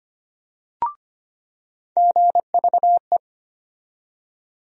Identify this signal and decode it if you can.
Morse code, heard as a keyed tone, about 25 wpm, 700 Hz. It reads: GVE
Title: USA Call Sign Postfixes - 3 Letters 25wpm